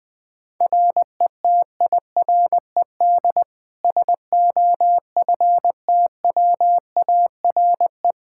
Morse code, heard as a keyed tone, about 20 wpm, 700 Hz. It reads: RETIRED SOFTWARE
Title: Practice QSO - Exchange 3 - More Info - 20wpm